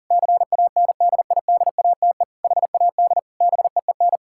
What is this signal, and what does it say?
Morse code, heard as a keyed tone, about 40 wpm, 700 Hz. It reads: CANDIDATE HAD BEEN
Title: Sentences from Top 1600 Words 40wpm